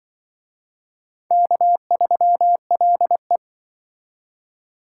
Morse code, heard as a keyed tone, about 25 wpm, 700 Hz. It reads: K3LE
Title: Speed Racing - USA 1x2 Call Signs 20wpm